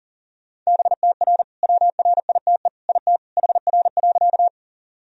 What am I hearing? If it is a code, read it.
Morse code, heard as a keyed tone, about 40 wpm, 700 Hz. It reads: BTR WRITE IT HR.
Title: Sentences from Top 400 Words and Abbreviations 40wpm